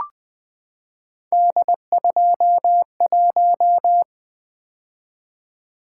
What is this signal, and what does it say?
Morse code, heard as a keyed tone, about 20 wpm, 700 Hz. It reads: D21
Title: Three Letter-Number Combinations - 20wpm